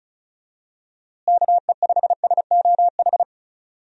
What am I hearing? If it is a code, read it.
Morse code, heard as a keyed tone, about 34 wpm, 700 Hz. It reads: KE5SOH
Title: USA Call Signs - 6 Characters - Repeated 3x - 35wpm